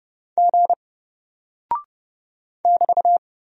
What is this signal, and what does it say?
Morse code, heard as a keyed tone, about 30 wpm, 700 Hz. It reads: G <BT>
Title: Single Character (Letter:Number:Punctuation:Prosign) 30wpm